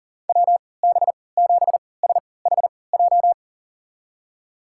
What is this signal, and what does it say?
Morse code, heard as a keyed tone, about 40 wpm, spaced as for 23 wpm, 700 Hz. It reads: WB7SHJ
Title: USA Call Signs - 6 Characters - 3x Character Spacing - 40wpm